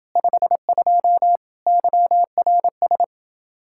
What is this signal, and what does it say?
Morse code, heard as a keyed tone, about 29 wpm, 700 Hz. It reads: 52 YRS